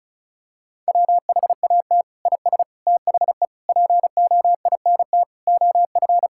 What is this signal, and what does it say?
Morse code, heard as a keyed tone, about 36 wpm, 700 Hz. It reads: WHAT IS THE POINT OF
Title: Sentences from Top 1000 Words - Review Part 5 - 35wpm